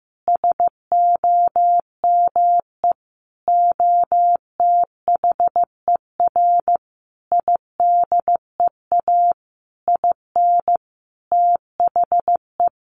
Morse code, heard as a keyed tone, about 15 wpm, 700 Hz. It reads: SOME OTHER IDEA IN THE
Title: Sentences from Top 1300 Words 15wpm